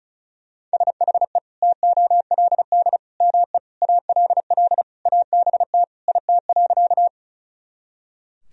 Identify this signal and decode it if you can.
Morse code, heard as a keyed tone, about 34 wpm, 700 Hz. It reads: SHE TOLD ME ALL ABT IT.
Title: Sentences from Top 400 Words and Abbreviations 35wpm